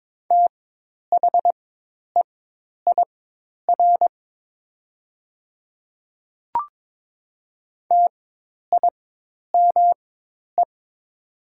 Morse code, heard as a keyed tone, about 22 wpm, spaced as for 10 wpm, 690 Hz. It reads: THEIR TIME
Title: 4x Character Spacing - Top 200 Words - 22wpm